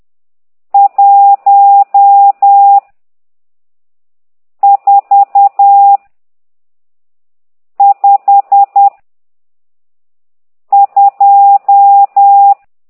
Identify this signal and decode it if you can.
Morse code, heard as a keyed tone, about 10 wpm, 800 Hz. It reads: 1 4 5 2
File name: ct2idw / exerc12